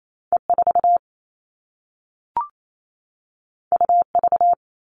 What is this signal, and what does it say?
Morse code, heard as a keyed tone, about 28 wpm, 700 Hz. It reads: E4 UV
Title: All Cut Numbers - 2-Digit 28wpm